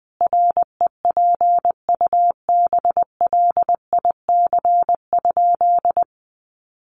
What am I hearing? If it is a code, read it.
Morse code, heard as a keyed tone, about 19 wpm, 700 Hz. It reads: REPUBLIC?
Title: Sentences from Top 1500 Words 20wpm